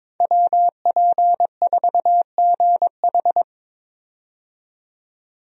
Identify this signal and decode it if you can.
Morse code, heard as a keyed tone, about 21 wpm, 700 Hz. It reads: WP4GH
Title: USA Call Signs - 5 Characters - 22wpm